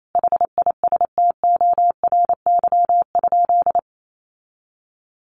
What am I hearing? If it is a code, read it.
Morse code, heard as a keyed tone, about 28 wpm, 700 Hz. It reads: HISTORY?
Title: Sentences from Top 1100 Words 28wpm